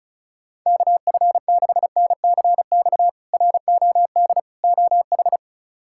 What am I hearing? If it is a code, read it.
Morse code, heard as a keyed tone, about 34 wpm, 700 Hz. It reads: KF6NCX ROD OH
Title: K1USN SST QSO Practice - 35wpm